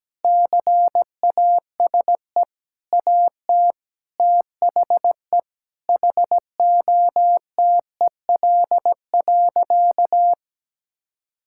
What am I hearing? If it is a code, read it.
Morse code, heard as a keyed tone, about 17 wpm, 700 Hz. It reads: CASE AT THE HOTEL.
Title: Sentences from Top 1600 Words 17wpm